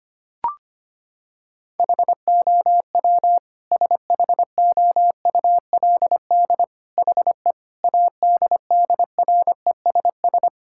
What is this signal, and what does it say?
Morse code, heard as a keyed tone, about 25 wpm, 700 Hz. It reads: HOW SHOULD HE ADDRESS
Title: Sentences from Top 1000 Words - Review Part 6 - 25wpm